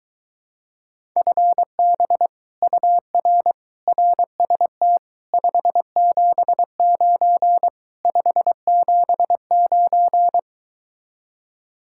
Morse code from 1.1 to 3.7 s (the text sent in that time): FB UR